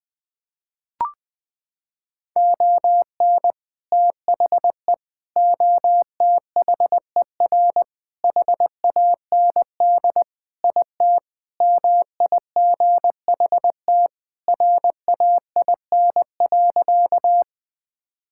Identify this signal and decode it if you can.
Morse code, heard as a keyed tone, about 20 wpm, 700 Hz. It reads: ON THE OTHER HAND IT MIGHT RAIN.
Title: Sentences with 3-Word Phrases - 20wpm